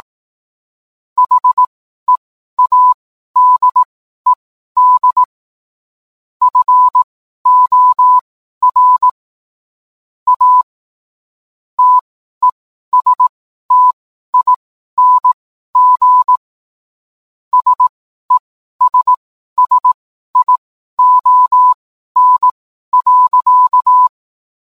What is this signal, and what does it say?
Morse code, heard as a keyed tone, about 18 wpm, spaced as for 12 wpm, 1000 Hz. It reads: HEADED FOR A TESTING SESSION.